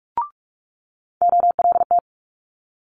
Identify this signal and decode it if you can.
Morse code, heard as a keyed tone, about 46 wpm, 700 Hz. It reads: OLT
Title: Multi-letter Phonemes Quick Fire 45wpm